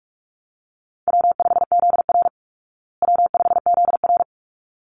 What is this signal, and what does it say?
Morse code, heard as a keyed tone, about 44 wpm, 700 Hz. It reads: W5ZR W5ZR
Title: Easy Call Signs Repeated 2x - 45wpm